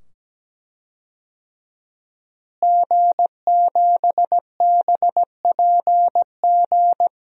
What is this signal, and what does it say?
Morse code, heard as a keyed tone, about 17 wpm, 700 Hz. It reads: G7BPG
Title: DXCC-100 Call Signs - 5 Characters - Repeated 2x - 17wpm